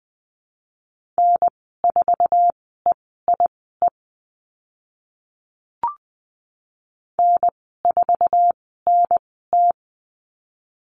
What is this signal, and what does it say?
Morse code, heard as a keyed tone, about 20 wpm, spaced as for 14 wpm, 700 Hz. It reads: N4EIE N4NT
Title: 062 Lesson 11g - Letter h - Call Signs 20wpm